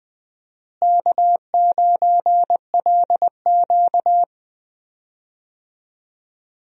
Morse code, heard as a keyed tone, about 20 wpm, 700 Hz. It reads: K9LQ
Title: USA Call Signs - 4 Characters - Repeated 3x - 20wpm